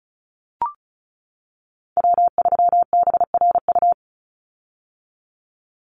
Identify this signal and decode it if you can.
Morse code, heard as a keyed tone, about 34 wpm, 700 Hz. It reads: W3BRU
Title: USA Call Signs 35wpm